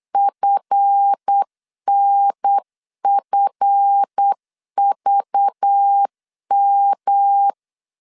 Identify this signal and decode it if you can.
Morse code, heard as a keyed tone, about 8 wpm, 800 Hz. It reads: FNFVM
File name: LEZIONI / 7 GF9Z / 7 GF 8wpm 3108